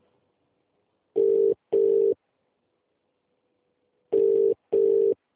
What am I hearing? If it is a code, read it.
Ringback tone (UK), 400 + 450 Hz.